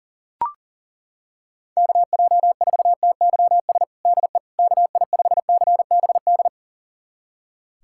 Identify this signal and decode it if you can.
Morse code, heard as a keyed tone, about 40 wpm, 700 Hz. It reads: KJ4TYS DE KI5CBD
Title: USA Callsign DE Callsign - 6 Characters - 40wpm